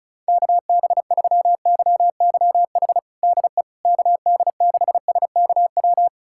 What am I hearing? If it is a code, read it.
Morse code, heard as a keyed tone, about 34 wpm, 700 Hz. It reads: KB3YYH DE KD6SKW
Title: USA Callsign DE Callsign - 6 Characters - 35wpm